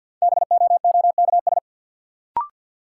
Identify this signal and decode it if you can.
Morse code, heard as a keyed tone, about 53 wpm, 700 Hz. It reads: BOOKS
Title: Sentences from Top 1200 Words - New Word Drill Alt - 50wpm